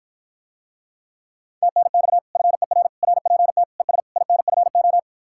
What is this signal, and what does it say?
Morse code, heard as a keyed tone, about 54 wpm, 700 Hz. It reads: TNX FER RPT ES INFO